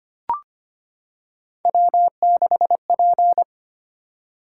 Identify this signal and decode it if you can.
Morse code, heard as a keyed tone, about 25 wpm, 700 Hz. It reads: W6P
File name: Three Letter-Number Combinations - 25wpm